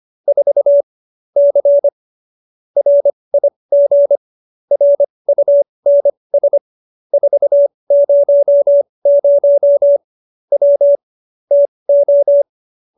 Morse code, heard as a keyed tone, about 25 wpm, spaced as for 20 wpm, 570 Hz. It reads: 4 C RIG RUNS 400 W TO